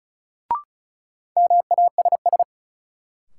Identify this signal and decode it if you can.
Morse code, heard as a keyed tone, about 36 wpm, 700 Hz. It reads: MASS
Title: Sets of 5 Words Encore - New Word Drill - 35wpm